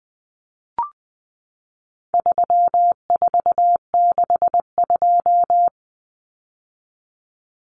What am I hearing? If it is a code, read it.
Morse code, heard as a keyed tone, about 20 wpm, 700 Hz. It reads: 3462
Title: Common Cut Numbers - 4-Digit 20wpm